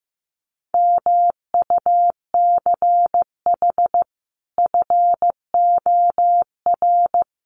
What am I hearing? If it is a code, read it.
Morse code, heard as a keyed tone, about 15 wpm, 700 Hz. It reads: MUCH FOR
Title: Sets of 5 Words Encore 15wpm